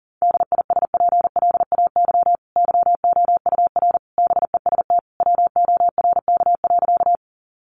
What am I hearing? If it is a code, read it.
Morse code, heard as a keyed tone, about 40 wpm, 700 Hz. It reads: DISPLAY YOUR BEST WORK.
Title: Sentences from Top 1200 Words 40wpm